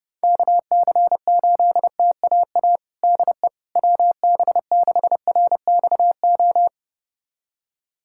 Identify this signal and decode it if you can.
Morse code, heard as a keyed tone, about 30 wpm, 700 Hz. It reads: KC8TAA DE WB6RXO
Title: USA Callsign DE Callsign - 6 Characters - 30wpm